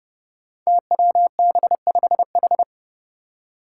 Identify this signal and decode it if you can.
Morse code, heard as a keyed tone, about 31 wpm, 700 Hz. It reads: TWB5H
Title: Five Challenging Letter-Number Combinations 30wpm